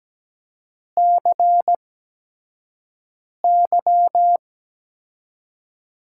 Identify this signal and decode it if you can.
Morse code, heard as a keyed tone, about 17 wpm, spaced as for 5 wpm, 700 Hz. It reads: CY
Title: USA Call Signs - 5 Characters - 8x Character Spacing - 17wpm